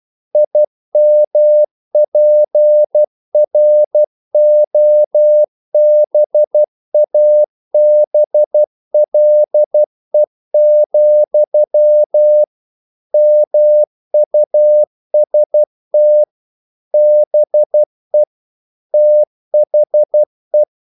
Morse code, heard as a keyed tone, about 12 wpm, 600 Hz. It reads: IMPROBABLE, MUST BE THE
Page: English quotations in CW (telegraphy - morse)